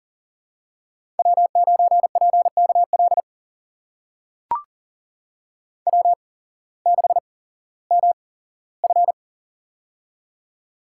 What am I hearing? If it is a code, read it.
Morse code, heard as a keyed tone, about 42 wpm, spaced as for 19 wpm, 700 Hz. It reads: W9PKL W6MF
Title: USA Call Signs - 8x Character Spacing - 40wpm